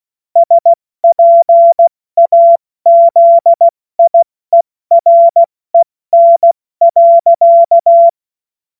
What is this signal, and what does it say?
Morse code, heard as a keyed tone, about 16 wpm, 680 Hz. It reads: SPAZIEREN.
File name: Morsekurs / Witze / Witz 13. April 2021, 70 F80